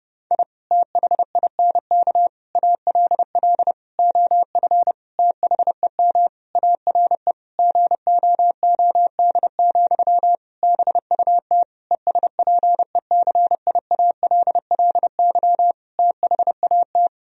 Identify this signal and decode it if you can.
Morse code, heard as a keyed tone, about 32 wpm, 700 Hz. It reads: I THINK ALL OF THEM ARE GOOD, BUT ESPECIALLY THAT
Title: Sentences from Top 1000 Words - Review Part 4 - 30wpm